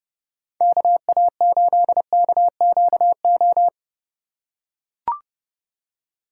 Morse code, heard as a keyed tone, about 31 wpm, 700 Hz. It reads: KA8KQO
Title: USA Call Signs - 6 Characters - 4x Character Spacing - 30wpm